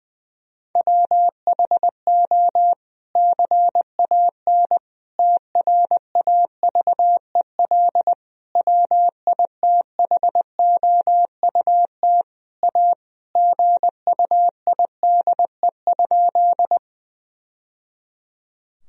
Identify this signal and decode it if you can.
Morse code, heard as a keyed tone, about 20 wpm, 700 Hz. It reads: WHO CAN TRAVEL WITHOUT A GUIDE?
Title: Sentences from Top 1000 Words - Review Part 3 - 20wpm